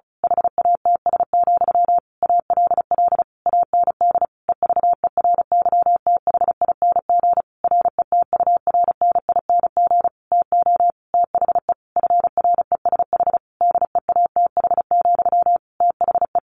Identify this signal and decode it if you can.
Morse code, heard as a keyed tone, about 34 wpm, 700 Hz. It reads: HATS, ALL AND EVERYTHING RETURNING TO THE FRESH DEATH, THE
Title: A Sound Of Thunder 35wpm